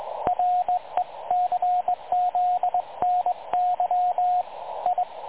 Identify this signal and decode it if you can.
Morse code, heard as a keyed tone, about 19 wpm, 720 Hz, with radio noise. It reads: RECZNY I
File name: dlazaawansowanych / nagraniazpasma / websdr recording 3565.0kHz